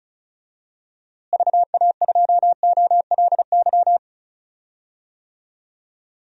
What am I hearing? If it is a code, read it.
Morse code, heard as a keyed tone, about 36 wpm, 700 Hz. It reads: VA2OLY
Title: DXCC-100 Call Signs - 6 Characters - Repeated 2x - 35wpm